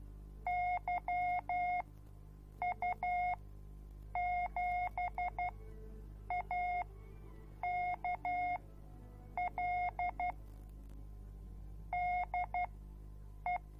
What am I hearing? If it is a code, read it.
Morse code, heard as a keyed tone, about 12 wpm, spaced as for 7 wpm, 720 Hz. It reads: YU7AKL DE